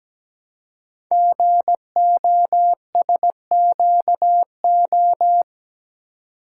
Morse code, heard as a keyed tone, about 17 wpm, 700 Hz. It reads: GOSQO
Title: Code Groups - 5 Letters - 17wpm